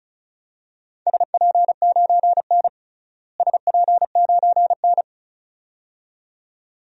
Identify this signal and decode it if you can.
Morse code, heard as a keyed tone, about 34 wpm, 700 Hz. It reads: SP9N SP9N
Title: DXCC-100 Call Signs - 4 Characters - Repeated 2x - 35wpm